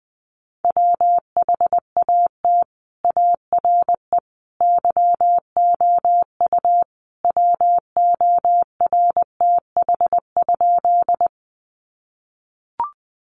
Morse code, heard as a keyed tone, about 20 wpm, 700 Hz. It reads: WHAT ARE YOU WORTH?
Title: Sentences from Top 800 Words 20wpm